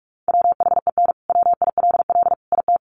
Morse code, heard as a keyed tone, about 44 wpm, 700 Hz. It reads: WHEN WILL IT
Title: Sentences from Top 200 Words 45wpm